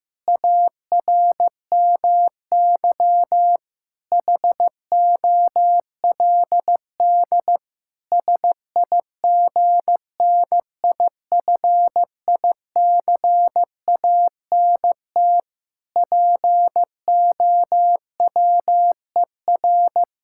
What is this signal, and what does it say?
Morse code, heard as a keyed tone, about 15 wpm, 700 Hz. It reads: ARMY HOLD SIGNIFICANT POWER
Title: Sentences from Top 1000 Words - Review Part 6 - 15wpm